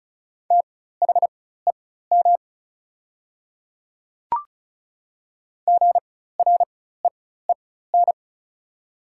Morse code, heard as a keyed tone, about 34 wpm, spaced as for 16 wpm, 690 Hz. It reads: THEM GREEN
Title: 4x Character Spacing - Top 500 Words - 35wpm